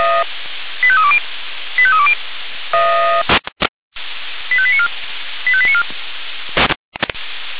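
Five-tone selective call (ZVEI), heard as quick runs of five tones, tones of about 70 ms. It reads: 85320 85320 85894 85894
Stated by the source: SDR receiver recording of ZVEI five-tone selective call 85894.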